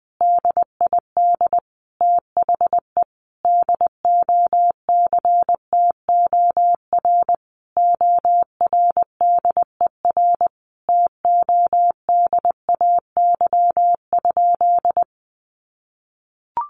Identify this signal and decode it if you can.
Morse code, heard as a keyed tone, about 20 wpm, 700 Hz. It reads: DID THE DOCTOR ORDER TODAY?
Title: Sentences from Top 1500 Words 20wpm